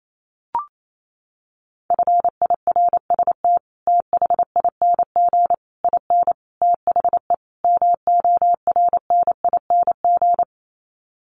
Morse code, heard as a keyed tone, about 28 wpm, 700 Hz. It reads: FIRST THING IN THE MORNING
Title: Sets of 5 Words Encore2 28wpm